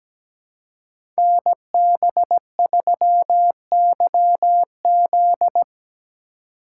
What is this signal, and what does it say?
Morse code, heard as a keyed tone, about 17 wpm, 700 Hz. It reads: NB3YZ
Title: Code Groups - 5 Letter:Numbers - 17wpm